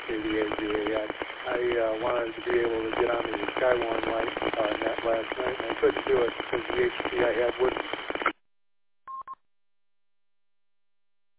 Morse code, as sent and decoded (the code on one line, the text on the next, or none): none